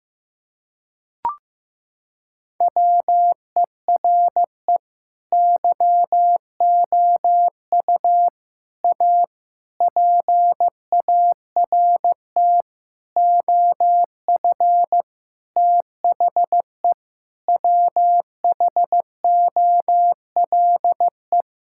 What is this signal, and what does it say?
Morse code, heard as a keyed tone, about 15 wpm, 700 Hz. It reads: WERE YOU A PART OF THE WHOLE
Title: Sentences from Top 1000 Words - Review Part 6 - 15wpm